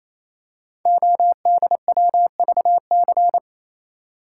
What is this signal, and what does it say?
Morse code, heard as a keyed tone, about 27 wpm, 700 Hz. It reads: ODWVC
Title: Code Groups - 5 Letters - 28wpm